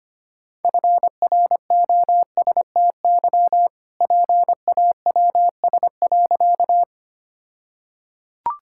Morse code, heard as a keyed tone, about 25 wpm, 700 Hz. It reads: FROSTY PAWS.
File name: practice / 25wpm / All-Age Appropriate Jokes - 25wpm